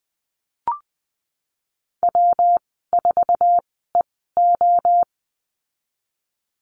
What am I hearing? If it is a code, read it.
Morse code, heard as a keyed tone, about 20 wpm, spaced as for 15 wpm, 700 Hz. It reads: W4EO
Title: 128 Lesson 21g - Number 6 - Call Signs 20wpm